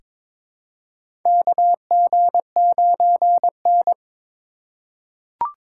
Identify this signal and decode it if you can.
Morse code, heard as a keyed tone, about 22 wpm, 700 Hz. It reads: KG9N
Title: USA Call Signs - 4 Characters - 2x Character Spacing - 22wpm